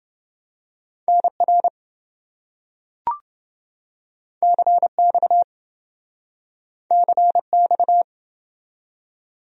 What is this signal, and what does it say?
Morse code, heard as a keyed tone, about 28 wpm, 700 Hz. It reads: NR CX CX